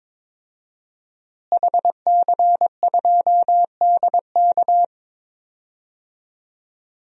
Morse code, heard as a keyed tone, about 22 wpm, 700 Hz. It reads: HC2DK
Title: DXCC-100 Call Signs - 5 Characters - Repeated 3x - 22wpm